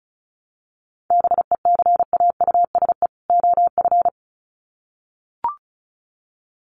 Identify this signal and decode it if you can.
Morse code, heard as a keyed tone, about 34 wpm, 700 Hz. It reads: BECAUSE OF
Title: Sets of 2 Words 35wpm